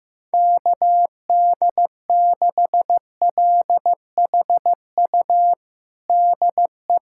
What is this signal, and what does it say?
Morse code, heard as a keyed tone, about 15 wpm, 700 Hz. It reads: KD6LHU DE
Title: USA Callsign DE Callsign 15wpm